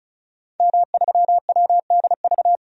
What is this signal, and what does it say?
Morse code, heard as a keyed tone, about 36 wpm, 700 Hz. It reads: M3WDV
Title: DXCC-100 Call Signs - 5 Characters - 35wpm